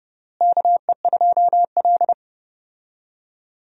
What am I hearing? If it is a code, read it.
Morse code, heard as a keyed tone, about 32 wpm, 700 Hz. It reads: KE2L
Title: USA Call Signs - 4 Characters - Repeated 3x - 30wpm